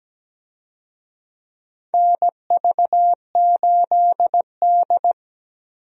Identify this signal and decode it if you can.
Morse code, heard as a keyed tone, about 17 wpm, 700 Hz. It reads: NV8D
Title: USA Call Signs - 4 Characters - 17wpm